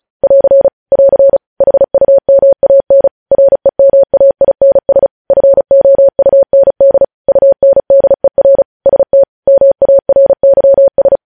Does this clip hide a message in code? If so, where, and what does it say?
Yes: Morse code from 0.2 to 11.2 s, heard as a keyed tone, about 36 wpm, 550 Hz. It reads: <AR> <AR> HUMAN REMAINS FOUND UNDER ST MARYS